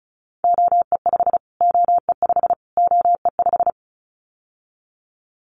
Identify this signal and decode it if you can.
Morse code, heard as a keyed tone, about 34 wpm, 700 Hz. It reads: OE5 OE5 OE5
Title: Callsign Prefixes for Top 25 DXCC - Repeated 3x - 35wpm